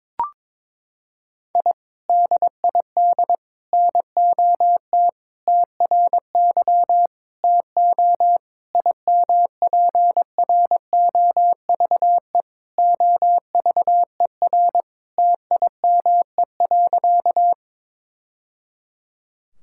Morse code, heard as a keyed tone, about 22 wpm, 700 Hz. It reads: I DID NOT TRY TO IMPROVE OVER TIME.
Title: Sentences from Top 1000 Words - Review Part 5 - 22wpm